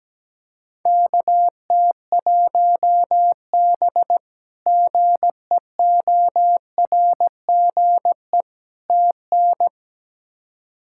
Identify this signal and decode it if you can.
Morse code, heard as a keyed tone, about 17 wpm, 700 Hz. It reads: KT1B GEORGE TN
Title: K1USN SST QSO Practice - 17wpm